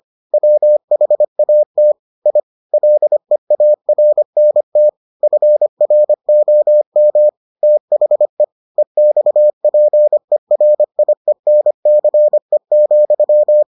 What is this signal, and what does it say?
Morse code, heard as a keyed tone, about 25 wpm, 600 Hz. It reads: WHAT I LEARNT FROM THE EXPERIENCE,